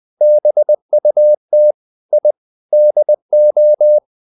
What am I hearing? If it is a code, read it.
Morse code, heard as a keyed tone, about 20 wpm, 600 Hz. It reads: BUT I DO
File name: the old man and the sea part 2 20wpm